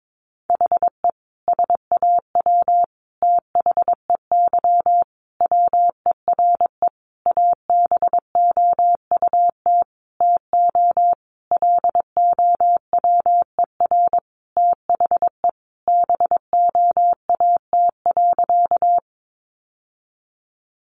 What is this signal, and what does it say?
Morse code, heard as a keyed tone, about 22 wpm, 700 Hz. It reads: HE SAW THEY WERE ABOUT TO LOWER THE BOAT.
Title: Sentences from Top 700 Words 22wpm